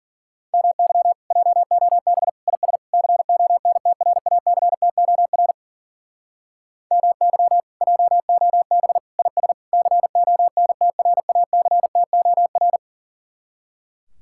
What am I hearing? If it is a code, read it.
Morse code, heard as a keyed tone, about 48 wpm, 700 Hz. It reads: MY JOB IS CONTRACTOR MY JOB IS CONTRACTOR
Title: Speed-Racing Common QSO Phrases - 40wpm